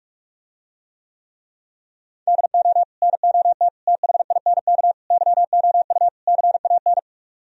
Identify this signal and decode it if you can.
Morse code, heard as a keyed tone, about 48 wpm, 700 Hz. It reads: DO NOT THINK YOU CAN